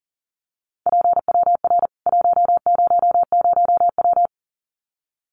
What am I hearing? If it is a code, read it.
Morse code, heard as a keyed tone, about 40 wpm, 700 Hz. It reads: PWR 100W
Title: Common QSO Phrases 40wpm